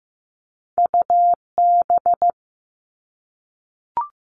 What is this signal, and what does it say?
Morse code, heard as a keyed tone, about 15 wpm, 700 Hz. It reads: UB
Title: Two Challenging Letter-Number Combinations 15wpm